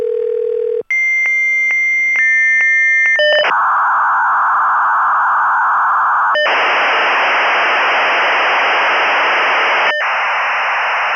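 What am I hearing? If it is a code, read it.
Dial-up modem handshake.
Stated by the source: Computer modem dialing and connecting, preceded by telephone rings and line noise.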